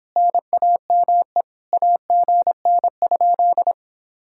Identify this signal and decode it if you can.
Morse code, heard as a keyed tone, about 29 wpm, 700 Hz. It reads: NAME AGN?